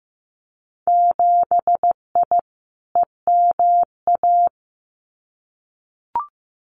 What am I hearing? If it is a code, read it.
Morse code, heard as a keyed tone, about 15 wpm, 700 Hz. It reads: 7I EMA
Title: Winter Field Day Exchanges Encore 15wpm